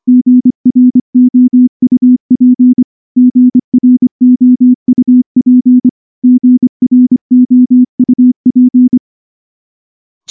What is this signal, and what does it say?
Morse code, heard as a keyed tone, about 24 wpm, 250 Hz. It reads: GROUP GROUP GROUP
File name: morse-training-25WPM